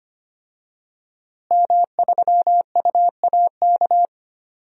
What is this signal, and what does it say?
Morse code, heard as a keyed tone, about 25 wpm, 700 Hz. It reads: M3UAK
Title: DXCC-100 Call Signs - 5 Characters - Repeated 2x - 25wpm